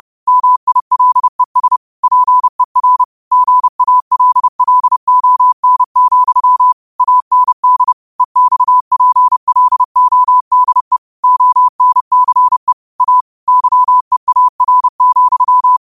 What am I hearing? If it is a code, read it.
Morse code, heard as a keyed tone, about 30 wpm, 1000 Hz. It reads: MILES PER GALLON, AND EXPLODE ONCE A YEAR,
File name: qotdfiles / 30wpm